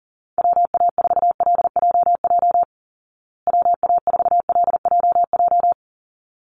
Morse code, heard as a keyed tone, about 40 wpm, 700 Hz. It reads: WA4LJJ WA4LJJ
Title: Call Signs Repeated 3x - 40wpm